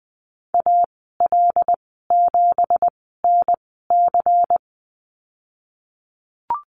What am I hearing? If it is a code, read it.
Morse code, heard as a keyed tone, about 20 wpm, spaced as for 14 wpm, 700 Hz. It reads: AL7NC
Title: 245 Lesson 39g - Letter z - Call Signs 20wpm